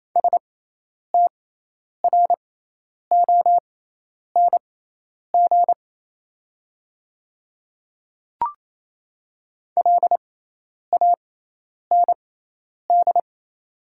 Morse code, heard as a keyed tone, about 28 wpm, spaced as for 10 wpm, 690 Hz. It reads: STRONG LAND
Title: 6x Character Spacing - Top 500 Words - 28wpm